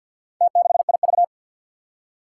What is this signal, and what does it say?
Morse code, heard as a keyed tone, about 48 wpm, 700 Hz. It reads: T6IV
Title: Four Challenging Letter-Number Combinations 50wpm